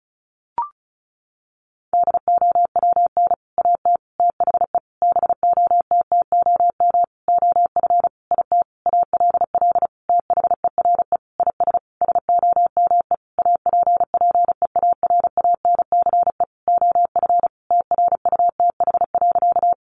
Morse code, heard as a keyed tone, about 34 wpm, 700 Hz. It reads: DOWN AT THE BOTTOM OF IT ALL THERE IS SOME APPEARANCE OF TRUTH.
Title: Sentences from Top 1100 Words 35wpm